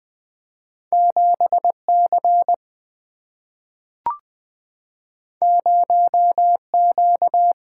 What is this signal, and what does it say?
Morse code, heard as a keyed tone, about 20 wpm, 700 Hz. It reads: 7C 0Q
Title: Single-Letter Single-Number 20wpm